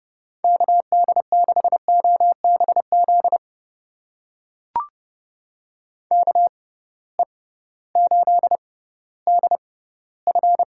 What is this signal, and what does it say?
Morse code, heard as a keyed tone, about 30 wpm, spaced as for 16 wpm, 700 Hz. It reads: KD6OBZ KE8DF
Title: USA Call Signs - 6 Characters - 6x Character Spacing - 30wpm